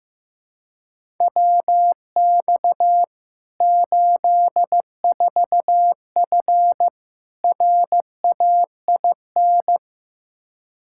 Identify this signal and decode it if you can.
Morse code, heard as a keyed tone, about 15 wpm, 700 Hz. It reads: WX 84F RAIN
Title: K1USN SST QSO Practice 15wpm